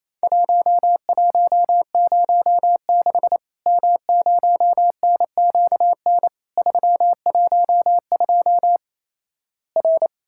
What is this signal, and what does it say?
Morse code, heard as a keyed tone, about 27 wpm, 700 Hz. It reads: 1106 M0NQN 312 R
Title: Worked All Europe DX Contest - QTC Exchange - 28wpm